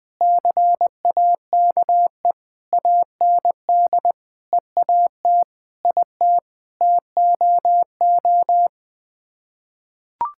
Morse code, heard as a keyed tone, about 20 wpm, 700 Hz. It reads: CAKE AND EAT IT TOO
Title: Sets of 8 Words - 20wpm